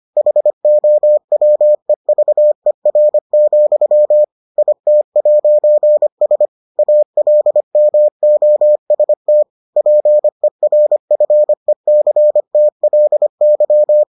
Morse code, heard as a keyed tone, about 25 wpm, 600 Hz. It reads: HOWEVER, IT'S ALMOST PERFECTLY